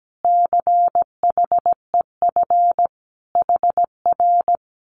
Morse code, heard as a keyed tone, about 17 wpm, 700 Hz. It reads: CHEF HR
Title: Common QSO Phrases Encore 17wpm